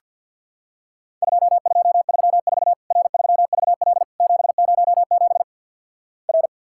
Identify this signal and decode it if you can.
Morse code, heard as a keyed tone, about 50 wpm, 700 Hz. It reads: 2234 R3VL 707 R
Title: Worked All Europe DX Contest - QTC Exchange with Repeats - 50wpm